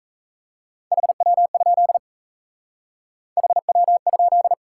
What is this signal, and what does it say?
Morse code, heard as a keyed tone, about 34 wpm, 700 Hz. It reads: HW? HW?
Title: Speed-Racing Top 180 QSO Elements 28wpm